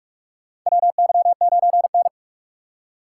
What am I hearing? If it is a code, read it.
Morse code, heard as a keyed tone, about 44 wpm, 700 Hz. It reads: WY9N
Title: USA Call Signs - 2x Character Spacing - 45wpm